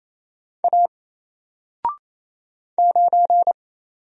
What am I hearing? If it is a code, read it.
Morse code, heard as a keyed tone, about 27 wpm, 700 Hz. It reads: A 9
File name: Single-Letter-Number-28wpm